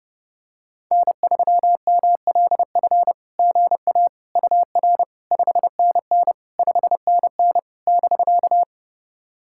Morse code, heard as a keyed tone, about 30 wpm, 700 Hz. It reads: N3MLF GA UR 5NN 5NN <BK>
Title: SOTA QSO Practice 30wpm